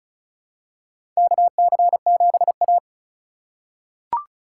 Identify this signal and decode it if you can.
Morse code, heard as a keyed tone, about 36 wpm, 700 Hz. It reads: KC7A
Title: USA Call Signs - 4 Characters - 2x Character Spacing - 35wpm